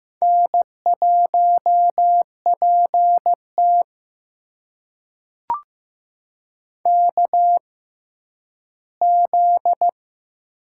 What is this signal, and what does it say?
Morse code, heard as a keyed tone, about 15 wpm, spaced as for 10 wpm, 700 Hz. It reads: N1PT KZ
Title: USA Call Signs - 4 Characters - 6x Character Spacing - 15wpm